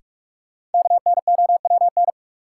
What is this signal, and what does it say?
Morse code, heard as a keyed tone, about 46 wpm, 700 Hz. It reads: KNOWN
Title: Sentences from Top 600 Words - New Word Drill - 45wpm